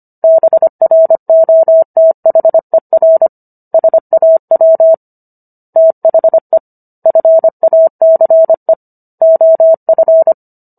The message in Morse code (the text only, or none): BROTHER SAW THE FACE OF